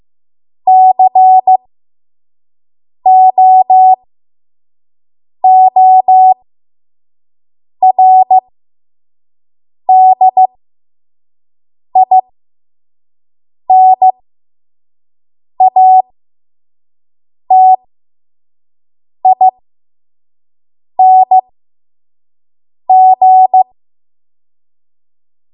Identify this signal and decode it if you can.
Morse code, heard as a keyed tone, about 15 wpm, spaced as for 5 wpm, 750 Hz. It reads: COORDINATING
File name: Morse / 05WPM